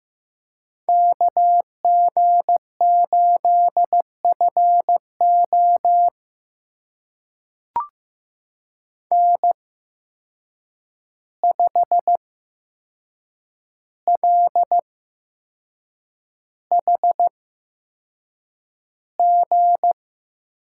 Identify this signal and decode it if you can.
Morse code, heard as a keyed tone, about 15 wpm, spaced as for 6 wpm, 700 Hz. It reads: KG8FO N5LHG
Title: USA Call Signs - 5 Characters - 8x Character Spacing - 15wpm